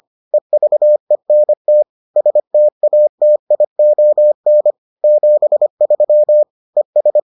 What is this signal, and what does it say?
Morse code, heard as a keyed tone, about 25 wpm, 600 Hz. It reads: EVENT STATION 73 ES